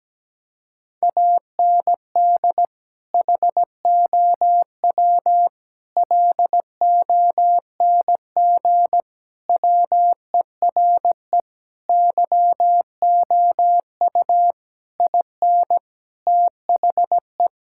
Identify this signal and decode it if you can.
Morse code, heard as a keyed tone, about 17 wpm, 700 Hz. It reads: AND HOW LONG WERE YOU IN THE
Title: Sentences from Top 1600 Words 17wpm